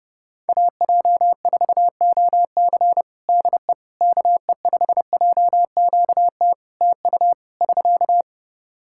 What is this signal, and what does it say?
Morse code, heard as a keyed tone, about 32 wpm, 700 Hz. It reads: AJ4OC DE KE5JQT TU <SK>
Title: K1USN SST QSO Practice 30wpm